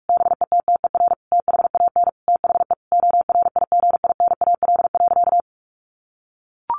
Morse code, heard as a keyed tone, about 44 wpm, 700 Hz. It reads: BETTER THAN THE ORIGINAL.
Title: Sentences from Top 1400 Words 45wpm